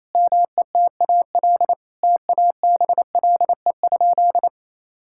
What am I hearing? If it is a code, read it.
Morse code, heard as a keyed tone, about 28 wpm, 700 Hz. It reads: METAL TABLE?
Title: Sentences from Top 1000 Words - Review Part 6 - 28wpm